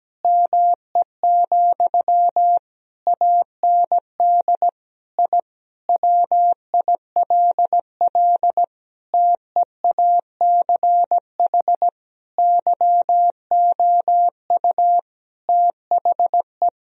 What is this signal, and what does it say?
Morse code, heard as a keyed tone, about 17 wpm, 700 Hz. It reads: ME, AND I WILL TEACH YOU THE
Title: Sentences from Top 1000 Words - Review Part 2 - 17wpm